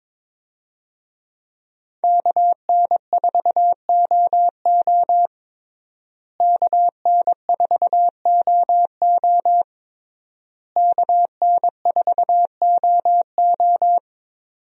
Morse code, heard as a keyed tone, about 22 wpm, 700 Hz. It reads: KN4OO KN4OO KN4OO
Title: USA Call Signs - 5 Characters - Repeated 3x - 22wpm